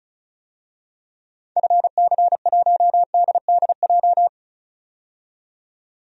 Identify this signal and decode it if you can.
Morse code, heard as a keyed tone, about 34 wpm, 700 Hz. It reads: FC1DDJ
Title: DXCC-100 Call Signs - 6 Characters - 35wpm